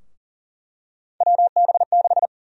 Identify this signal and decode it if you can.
Morse code, heard as a keyed tone, about 40 wpm, 700 Hz. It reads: WB6
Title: USA Call Sign Prefixes - 40wpm